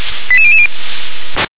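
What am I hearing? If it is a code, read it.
Five-tone selective call (ZVEI), heard as quick runs of five tones, tones of about 70 ms. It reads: 88000
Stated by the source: ZVEI five-tone selective call received on an SDR.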